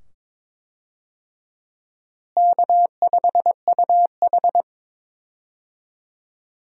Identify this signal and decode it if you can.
Morse code, heard as a keyed tone, about 22 wpm, 700 Hz. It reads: K5UH
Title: USA Call Signs - 4 Characters - Repeated 2x - 22wpm